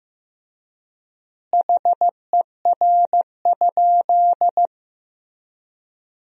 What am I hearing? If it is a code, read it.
Morse code, heard as a keyed tone, about 15 wpm, 700 Hz. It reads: HER?
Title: 3x Word-Spacing - Sentences from Top 200 Words - 15wpm